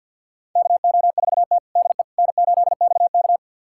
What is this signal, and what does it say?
Morse code, heard as a keyed tone, about 49 wpm, 700 Hz. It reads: KO4T DE N8XK
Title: USA Callsign DE Callsign - 4 Characters - 50wpm